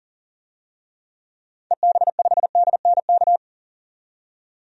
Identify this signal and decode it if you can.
Morse code, heard as a keyed tone, about 40 wpm, 700 Hz. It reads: EB5DNK
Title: DXCC-100 Call Signs - 6 Characters - 40wpm